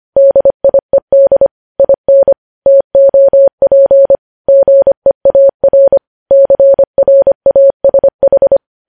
Morse code, heard as a keyed tone, about 25 wpm, 550 Hz. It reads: DIED IN TOP GEAR CRASH
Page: News Headlines in Morse Code at 25 WPM